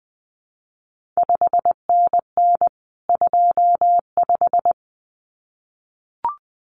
Morse code, heard as a keyed tone, about 20 wpm, 700 Hz. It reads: 5NN 25
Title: CQ WW DX Contest Exchange - 20wpm